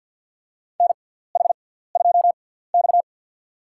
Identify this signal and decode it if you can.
Morse code, heard as a keyed tone, about 50 wpm, spaced as for 17 wpm, 700 Hz. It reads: NH2X
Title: USA Call Signs - 4 Characters - 6x Character Spacing - 50wpm